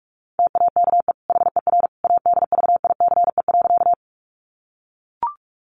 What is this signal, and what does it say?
Morse code, heard as a keyed tone, about 44 wpm, 700 Hz. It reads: TAKE HER ADVICE.
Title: Sentences from Top 1100 Words 45wpm